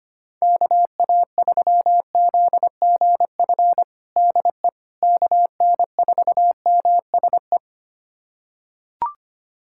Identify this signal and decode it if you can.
Morse code, heard as a keyed tone, about 25 wpm, 700 Hz. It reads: KA3ZGF DE KN4MSE
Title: USA Callsign DE Callsign 25wpm